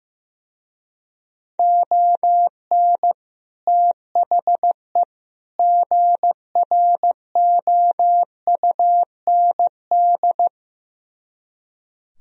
Morse code, heard as a keyed tone, about 15 wpm, 700 Hz. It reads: ON THE GROUND
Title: Speed-Racing - Sets of 3 Words 15wpm